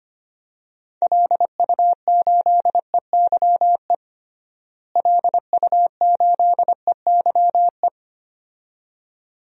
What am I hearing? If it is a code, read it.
Morse code, heard as a keyed tone, about 25 wpm, 700 Hz. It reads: LU8EYE LU8EYE
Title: DXCC-100 Call Signs - 6 Characters - Repeated 2x - 25wpm